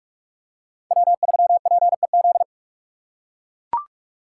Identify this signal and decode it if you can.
Morse code, heard as a keyed tone, about 41 wpm, 700 Hz. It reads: W3PEZ
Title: USA Call Signs - 5 Characters - 8x Character Spacing - 45wpm